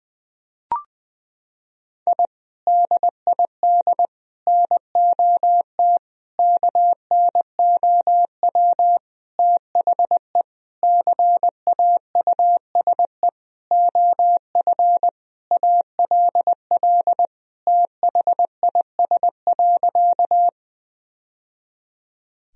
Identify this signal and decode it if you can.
Morse code, heard as a keyed tone, about 20 wpm, 700 Hz. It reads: I DID NOT KNOW THE CAUSE OF ALL THIS.
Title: Sentences from Top 1000 Words - Review Part 1 - 20wpm